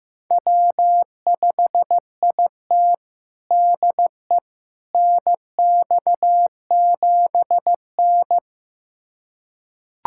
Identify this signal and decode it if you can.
Morse code, heard as a keyed tone, about 16 wpm, 700 Hz. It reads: W5IT DE NX7N
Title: USA Callsign DE Callsign - 4 Characters - 15wpm